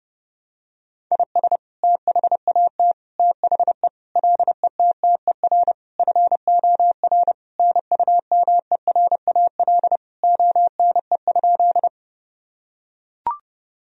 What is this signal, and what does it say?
Morse code, heard as a keyed tone, about 30 wpm, 700 Hz. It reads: IS THAT THE LETTER FOR NUMERAL ONE?
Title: Sentences from Top 1000 Words - Review Part 2 - 30wpm